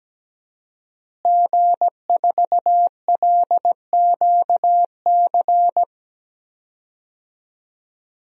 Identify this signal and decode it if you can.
Morse code, heard as a keyed tone, about 17 wpm, 700 Hz. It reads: G4LQC
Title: DXCC-100 Call Signs - 5 Characters - Repeated 2x - 17wpm